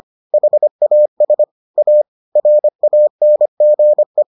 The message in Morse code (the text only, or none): HAS A RANGE